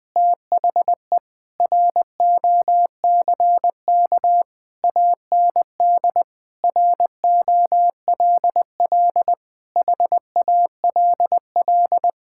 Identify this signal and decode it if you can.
Morse code, heard as a keyed tone, about 20 wpm, 700 Hz. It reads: THE ROCK AND ROLL HALL
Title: Sentences with 6-Word Phrases - 20wpm